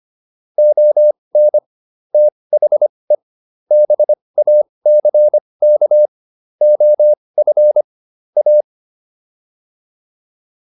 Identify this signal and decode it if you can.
Morse code, heard as a keyed tone, about 25 wpm, spaced as for 20 wpm, 600 Hz. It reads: ON THE BACK OF A